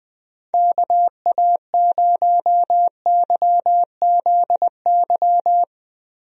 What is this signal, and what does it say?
Morse code, heard as a keyed tone, about 21 wpm, 700 Hz. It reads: KA0YZY
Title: USA Call Signs - 6 Characters - Repeated 3x - 20wpm